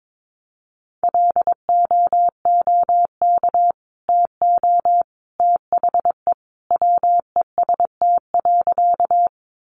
Morse code, heard as a keyed tone, about 22 wpm, 700 Hz. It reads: LOOK TO THE WEST.
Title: Sentences from Top 400 Words 22wpm